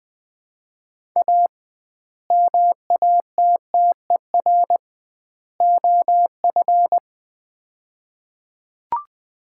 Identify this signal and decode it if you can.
Morse code, heard as a keyed tone, about 20 wpm, 700 Hz. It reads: A MATTER OF
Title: 2x Word-Spacing - Sets of 3 Words - 20wpm